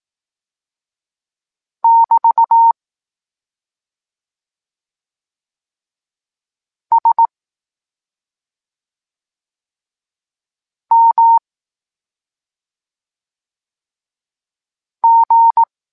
Morse code, heard as a keyed tone, about 18 wpm, 920 Hz. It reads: <BT> S M G
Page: Tekst 23.3 – Ciąg dalszy doskonalenia odbioru poznanych liter